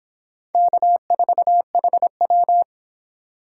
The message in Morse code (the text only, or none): K4HW